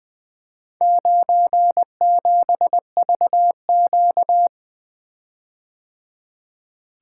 Morse code, heard as a keyed tone, about 24 wpm, 700 Hz. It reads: 97VQ
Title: Speed-Racing - Four Letter-Number Combinations - 20wpm